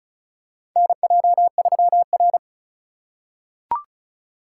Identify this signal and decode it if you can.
Morse code, heard as a keyed tone, about 36 wpm, 700 Hz. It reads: NJ3R
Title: USA Call Signs - 3x Character Spacing - 35wpm